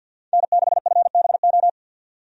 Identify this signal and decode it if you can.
Morse code, heard as a keyed tone, about 51 wpm, 700 Hz. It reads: N6FBO
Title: USA Call Signs - 5 Characters - 50wpm